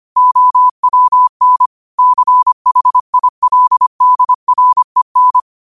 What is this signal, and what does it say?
Morse code, heard as a keyed tone, about 25 wpm, 1000 Hz. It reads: OWN CHILDREN